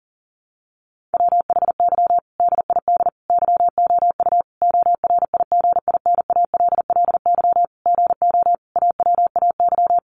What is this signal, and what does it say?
Morse code, heard as a keyed tone, about 40 wpm, 700 Hz. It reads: WHY DID YOU ORIGINALLY GO AWAY
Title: Sentences from Top 1500 Words 40wpm